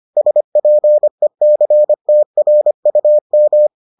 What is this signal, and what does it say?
Morse code, heard as a keyed tone, about 25 wpm, 600 Hz. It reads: SPECTRUM